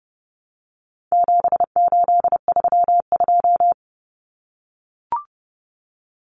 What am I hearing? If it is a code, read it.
Morse code, heard as a keyed tone, about 30 wpm, 700 Hz. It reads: 7832
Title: Common Cut Numbers - 4-Digit 30wpm